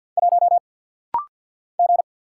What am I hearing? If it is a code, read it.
Morse code, heard as a keyed tone, about 50 wpm, 690 Hz. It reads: 1 G
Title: Single Letter-Number Alt - Rapid-Fire 50wpm